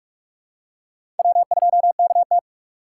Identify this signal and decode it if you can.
Morse code, heard as a keyed tone, about 46 wpm, 700 Hz. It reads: W2KT
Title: USA Call Signs - 4 Characters - 4x Character Spacing - 45wpm